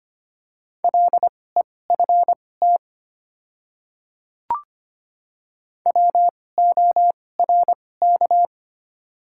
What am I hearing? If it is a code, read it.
Morse code, heard as a keyed tone, about 25 wpm, spaced as for 18 wpm, 700 Hz. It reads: LEFT WORK
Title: 2x Character Spacing - Top 300 Words - 25wpm